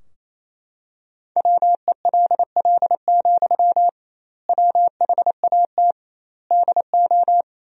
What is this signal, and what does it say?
Morse code, heard as a keyed tone, about 28 wpm, 700 Hz. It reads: WELL, WHAT DO
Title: 2x Word-Spacing - Sentences from Top 200 Words - 28wpm